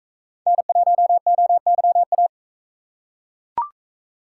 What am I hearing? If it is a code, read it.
Morse code, heard as a keyed tone, about 41 wpm, 700 Hz. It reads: N1OYA